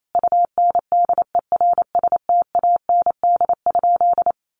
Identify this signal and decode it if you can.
Morse code, heard as a keyed tone, about 28 wpm, 700 Hz. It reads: UNDERSTAND?
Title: Sentences from Top 1100 Words 28wpm